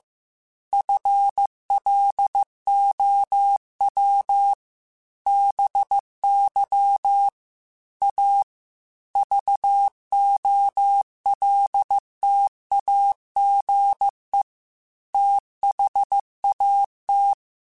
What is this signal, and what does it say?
Morse code, heard as a keyed tone, about 15 wpm, 800 Hz. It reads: FLOW BY A VOLTAGE THAT
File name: Technical Subjects / Technical Subjects-mp3s / XL @13